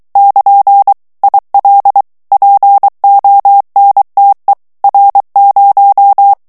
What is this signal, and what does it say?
Morse code, heard as a keyed tone, about 23 wpm, 800 Hz. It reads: <KN> IL PONTE R0
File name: morse mp3 / ITALIEN texte clair 1 vitesse 22 mots minute